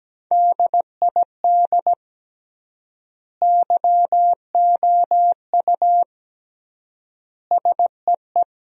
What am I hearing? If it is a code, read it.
Morse code, heard as a keyed tone, about 17 wpm, 700 Hz. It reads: DID YOU SEE
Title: 3x Word-Spacing - Sentences from Top 100 Words - 17wpm